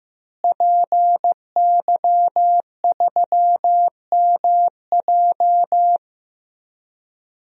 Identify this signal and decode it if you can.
Morse code, heard as a keyed tone, about 15 wpm, 700 Hz. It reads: PY3MJ
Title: DXCC-100 Call Signs - 5 Characters - Repeated 2x - 15wpm